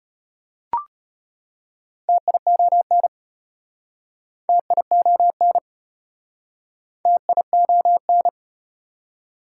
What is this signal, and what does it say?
Morse code, heard as a keyed tone, about 33 wpm, 700 Hz. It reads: TION TION TION